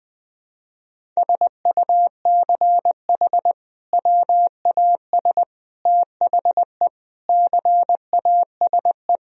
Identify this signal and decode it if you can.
Morse code, heard as a keyed tone, about 20 wpm, 700 Hz. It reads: SUCH WAS THE CASE
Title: Sentences from Top 1000 Words - Review Part 1 - 20wpm